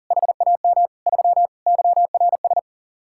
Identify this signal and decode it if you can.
Morse code, heard as a keyed tone, about 51 wpm, 700 Hz. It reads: HAM 3 YRS